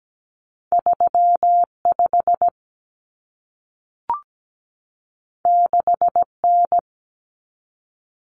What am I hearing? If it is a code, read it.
Morse code, heard as a keyed tone, about 17 wpm, 700 Hz. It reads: 35 6N
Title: Common Cut Numbers - 2-Digit 17wpm